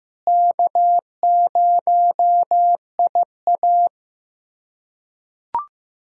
Morse code, heard as a keyed tone, about 15 wpm, 700 Hz. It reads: K0IA
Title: USA Callsign DE Callsign - 4 Characters - 15wpm